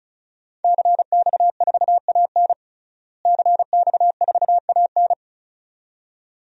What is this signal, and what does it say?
Morse code, heard as a keyed tone, about 36 wpm, 700 Hz. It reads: CX4AN CX4AN
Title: DXCC-100 Call Signs - 5 Characters - Repeated 2x - 35wpm